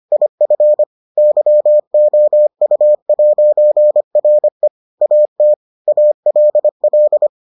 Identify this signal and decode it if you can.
Morse code, heard as a keyed tone, about 25 wpm, 600 Hz. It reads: IF YOU'RE AT ALL